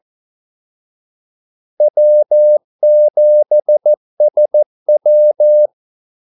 Morse code, heard as a keyed tone, about 14 wpm, 600 Hz. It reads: W7SW
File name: Callsigns-14wpm